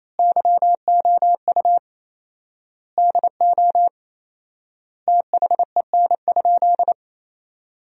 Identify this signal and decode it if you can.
Morse code, heard as a keyed tone, about 28 wpm, 700 Hz. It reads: YOU DO THEN?
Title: 4x Word-Spacing - Sentences from Top 100 Words - 28wpm